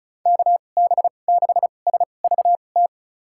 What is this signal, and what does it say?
Morse code, heard as a keyed tone, about 34 wpm, spaced as for 25 wpm, 700 Hz. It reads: KB6SVT
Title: USA Call Signs - 6 Characters - 2x Character Spacing - 35wpm